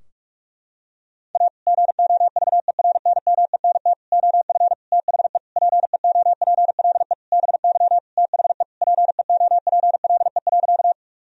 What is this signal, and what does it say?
Morse code, heard as a keyed tone, about 45 wpm, 700 Hz. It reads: A GOVERNMENT OF THE PEOPLE BY THE PEOPLE.
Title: Sentences from Top 1000 Words - Review Part 3 - 45wpm